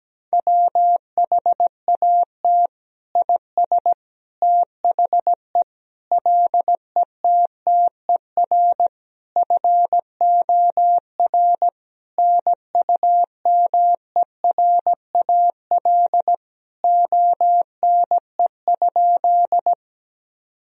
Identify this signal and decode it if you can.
Morse code, heard as a keyed tone, about 17 wpm, 700 Hz. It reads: WHAT IS THE LETTER FOR NUMERAL ONE?
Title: Sentences from Top 1000 Words - Review Part 2 - 17wpm